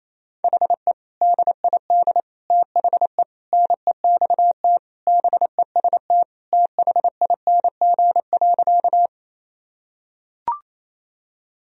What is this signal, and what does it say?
Morse code, heard as a keyed tone, about 27 wpm, 700 Hz. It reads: HE DID THE NEXT BEST THING.
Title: Sentences from Top 1000 Words - Review Part 1 - 28wpm